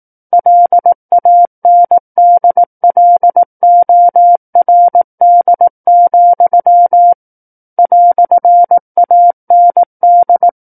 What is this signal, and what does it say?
Morse code, heard as a keyed tone, about 18 wpm, 700 Hz. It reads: LANDLORD, "AND
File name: 18 WPM